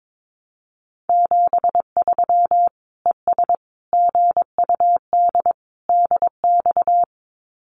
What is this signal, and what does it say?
Morse code, heard as a keyed tone, about 22 wpm, 700 Hz. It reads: 73 ES GUD DX
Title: Common QSO Phrases 22wpm